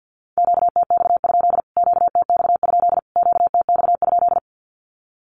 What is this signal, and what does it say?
Morse code, heard as a keyed tone, about 50 wpm, 700 Hz. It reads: QTX? QTX? QTX?
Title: Q-Codes Repeated 3x 50wpm